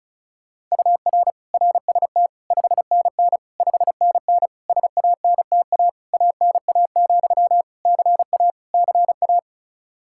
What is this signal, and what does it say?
Morse code, heard as a keyed tone, about 36 wpm, 700 Hz. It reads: UR RST 5NN 5NN SANTA ANA, CA CA
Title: Practice QSO - Exchange 1 - Beginning - 35wpm